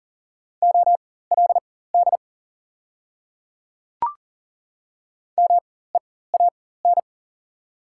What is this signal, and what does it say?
Morse code, heard as a keyed tone, about 40 wpm, spaced as for 19 wpm, 690 Hz. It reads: OLD MEAN
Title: 4x Character Spacing - Top 300 Words - 40wpm